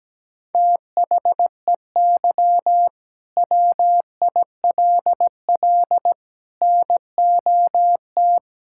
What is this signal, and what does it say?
Morse code, heard as a keyed tone, about 17 wpm, 700 Hz. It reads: THEY WILL NOT
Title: Sentences from Top 1000 Words - Review Part 5 - 17wpm